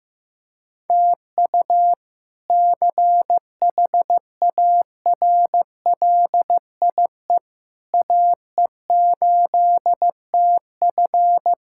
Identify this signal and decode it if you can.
Morse code, heard as a keyed tone, about 15 wpm, 700 Hz. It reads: TU CHARLIE AE8TF
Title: K1USN SST QSO Practice - 15wpm